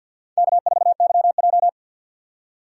Morse code, heard as a keyed tone, about 47 wpm, 700 Hz. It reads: K4YJ
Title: USA Call Signs - 4 Characters - 6x Character Spacing - 50wpm